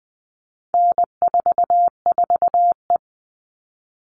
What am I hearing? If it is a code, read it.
Morse code, heard as a keyed tone, about 20 wpm, 700 Hz. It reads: N44E
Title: All Cut Numbers - 4-Digit 20wpm